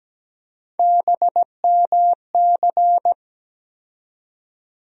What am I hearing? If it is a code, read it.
Morse code, heard as a keyed tone, about 17 wpm, 700 Hz. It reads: BMC
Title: USA Call Sign Postfixes - 3 Letters 17wpm